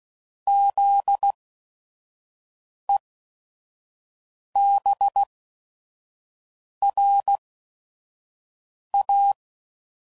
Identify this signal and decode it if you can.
Morse code, heard as a keyed tone, about 16 wpm, spaced as for 5 wpm, 800 Hz. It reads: ZEBRA
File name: Prefixes / By First Letter / Zs